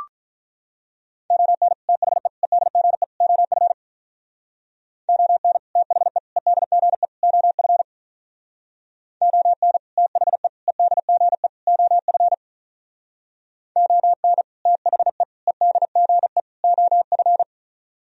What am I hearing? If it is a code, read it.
Morse code, heard as a keyed tone, about 46 wpm, 700 Hz. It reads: ON THE EDGE OF ON THE EDGE OF ON THE EDGE OF ON THE EDGE OF